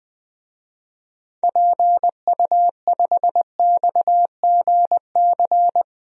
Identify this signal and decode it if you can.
Morse code, heard as a keyed tone, about 20 wpm, 700 Hz. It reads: PU5XGC
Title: DXCC-100 Call Signs - 6 Characters - 20wpm